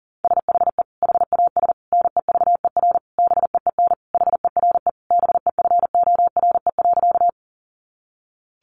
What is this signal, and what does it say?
Morse code, heard as a keyed tone, about 40 wpm, 700 Hz. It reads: SHE HAS NEVER BEEN HERE BEFORE.
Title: Additional Easy Phrases 40wpm